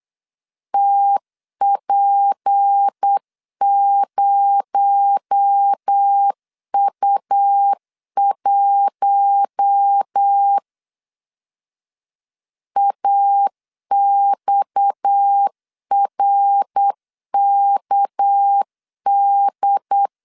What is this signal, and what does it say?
Morse code, heard as a keyed tone, about 8 wpm, 800 Hz. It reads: TP0U1 AXRKD
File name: CW / DET / 1 6 1500C S B 8wpm 5142